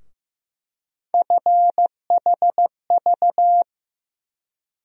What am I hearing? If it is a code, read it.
Morse code, heard as a keyed tone, about 15 wpm, 700 Hz. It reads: FHV
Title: USA Call Sign Postfixes - 3 Letters 15wpm